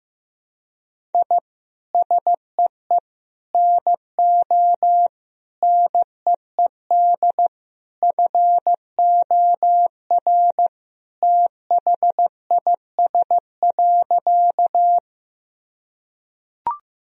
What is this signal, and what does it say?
Morse code, heard as a keyed tone, about 15 wpm, 700 Hz. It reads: I SEE NO NEED FOR THIS.
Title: Sentences from Top 1000 Words - Review Part 1 - 15wpm